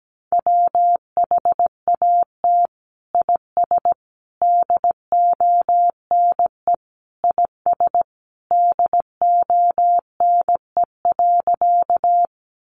Morse code, heard as a keyed tone, about 17 wpm, 700 Hz. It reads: WHAT IS DONE IS DONE.
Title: Sentences from Top 500 Words 17wpm